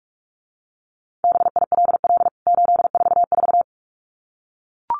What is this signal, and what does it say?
Morse code, heard as a keyed tone, about 44 wpm, 700 Hz. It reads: BILL 844
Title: CWT Exchange 45wpm